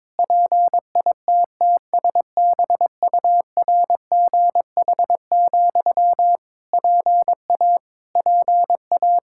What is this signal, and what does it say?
Morse code, heard as a keyed tone, about 22 wpm, 700 Hz. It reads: PITTSBURGH, PA PA
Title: Practice QSO - Exchange 1 - Beginning - 22wpm